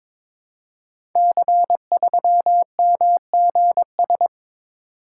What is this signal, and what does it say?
Morse code, heard as a keyed tone, about 22 wpm, 700 Hz. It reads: C3MGS
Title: Code Groups - 5 Letter:Numbers - 22wpm